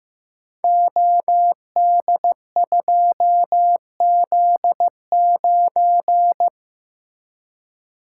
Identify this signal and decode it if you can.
Morse code, heard as a keyed tone, about 15 wpm, 700 Hz. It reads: OD2Z9
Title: Code Groups - 5 Letter:Numbers - 15wpm